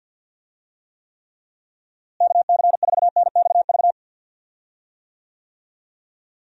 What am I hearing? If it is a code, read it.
Morse code, heard as a keyed tone, about 46 wpm, 700 Hz. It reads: KC4NXV
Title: USA Call Signs - 6 Characters - Repeated 3x - 50wpm